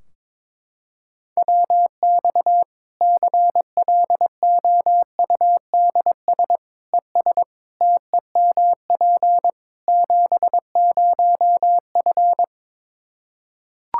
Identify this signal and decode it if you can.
Morse code, heard as a keyed tone, about 22 wpm, 700 Hz. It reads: WX CLOUDS ES TEMP 70F
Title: Practice QSO - Exchange 2 - Info - 22wpm